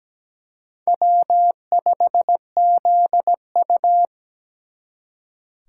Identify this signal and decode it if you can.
Morse code, heard as a keyed tone, about 20 wpm, 700 Hz. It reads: W5ZU
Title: Speed Racing - USA Call Signs 17wpm